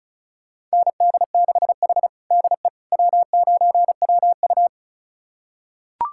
Morse code, heard as a keyed tone, about 36 wpm, 700 Hz. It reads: ND6H DE W9WU
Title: USA Callsign DE Callsign - 4 Characters - 35wpm